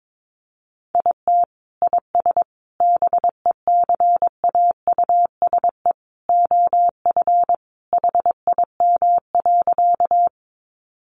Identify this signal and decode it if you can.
Morse code, heard as a keyed tone, about 22 wpm, 700 Hz. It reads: IT IS BECAUSE OF HIM.
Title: Sentences from Top 600 Words 22wpm